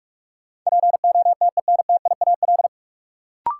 Morse code, heard as a keyed tone, about 43 wpm, 700 Hz. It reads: POTENTIAL
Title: Sentences from Top 1000 Words - Review Part 4 - New Word Drill - 45wpm